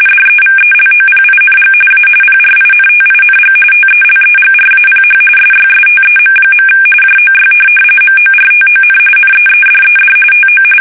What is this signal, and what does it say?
RTTY, about 45.45 baud. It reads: DEVELOPING A PRINTING TELEGRAPH SYSTEM. JOY MORTON NEEDED TO
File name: fsk850 / teletype-corp